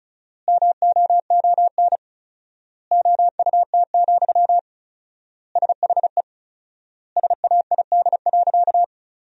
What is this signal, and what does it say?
Morse code, heard as a keyed tone, about 34 wpm, 700 Hz. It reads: MOON OUT, SHE SAID.
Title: 4x Word-Spacing - Sentences from Top 500 Words - 35wpm